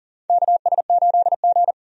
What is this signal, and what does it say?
Morse code, heard as a keyed tone, about 41 wpm, 690 Hz. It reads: KS8G
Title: USA Call Signs - 4 Characters - 40wpm